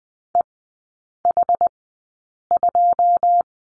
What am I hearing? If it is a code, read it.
Morse code, heard as a keyed tone, about 20 wpm, spaced as for 8 wpm, 690 Hz. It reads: EH2
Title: 159 Lesson 26e - Letter g - Three Characters 20wpm